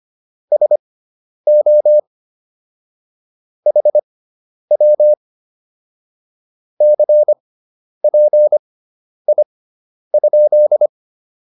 Morse code, heard as a keyed tone, about 25 wpm, spaced as for 10 wpm, 600 Hz. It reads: SO HW CPI?
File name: QSO201 10